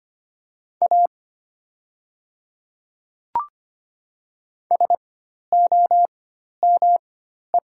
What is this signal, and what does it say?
Morse code, heard as a keyed tone, about 25 wpm, spaced as for 12 wpm, 700 Hz. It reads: A SOME
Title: 4x Character Spacing - Top 100 Words - 25wpm